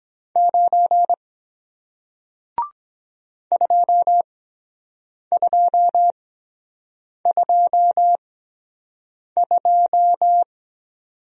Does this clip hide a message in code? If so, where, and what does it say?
Yes: Morse code from 0.4 to 10.4 s, heard as a keyed tone, about 23 wpm, spaced as for 7 wpm, 700 Hz. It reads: 9 2222